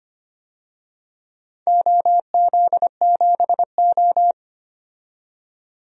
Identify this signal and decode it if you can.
Morse code, heard as a keyed tone, about 25 wpm, 700 Hz. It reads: OZ7O
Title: DXCC-100 Call Signs - 4 Characters - 25wpm